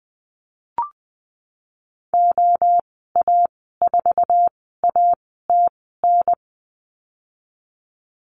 Morse code, heard as a keyed tone, about 20 wpm, spaced as for 14 wpm, 700 Hz. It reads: OA4ATN
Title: 048 Lesson 9g - Number 4 - Call Signs 20wpm